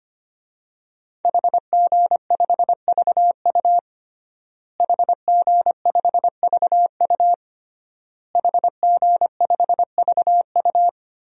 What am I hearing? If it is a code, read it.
Morse code, heard as a keyed tone, about 25 wpm, 700 Hz. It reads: HG5VU HG5VU HG5VU
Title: DXCC-100 Call Signs - 5 Characters - Repeated 3x - 25wpm